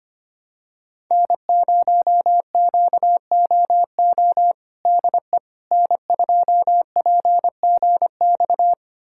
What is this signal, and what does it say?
Morse code, heard as a keyed tone, about 25 wpm, 700 Hz. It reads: N0QOO DE N2PGX
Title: USA Callsign DE Callsign - 5 Characters - 25wpm